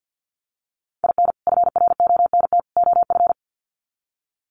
Morse code, heard as a keyed tone, about 50 wpm, 700 Hz. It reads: IN FRONT OF
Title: Sets of 3 Words 50wpm